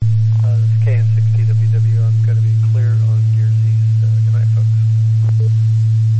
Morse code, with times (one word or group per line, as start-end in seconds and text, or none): none